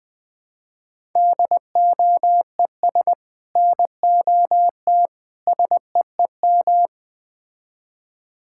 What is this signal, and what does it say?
Morse code, heard as a keyed tone, about 20 wpm, 700 Hz. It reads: DOES NOT SEEM
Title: Build A Phrase - 5-Word Phrases - 20wpm